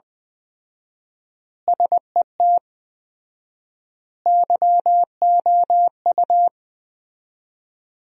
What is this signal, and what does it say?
Morse code, heard as a keyed tone, about 20 wpm, 700 Hz. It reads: SET YOU
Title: 4x Word-Spacing - Sentences from Top 200 Words - 20wpm